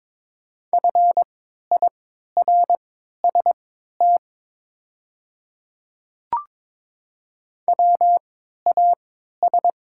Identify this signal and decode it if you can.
Morse code, heard as a keyed tone, about 22 wpm, spaced as for 12 wpm, 700 Hz. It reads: FIRST WAS
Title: 3x Character Spacing - Top 100 Words - 22wpm